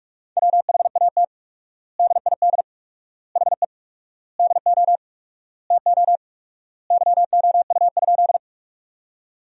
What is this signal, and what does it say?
Morse code, heard as a keyed tone, about 46 wpm, 700 Hz. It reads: WHAT DID HE DO TO YOU?
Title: 4x Word-Spacing - Sentences from Top 100 Words - 45wpm